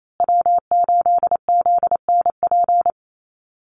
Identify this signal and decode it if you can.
Morse code, heard as a keyed tone, about 28 wpm, 700 Hz. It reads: W8ZNP
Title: USA Call Signs Repeated 3x 28wpm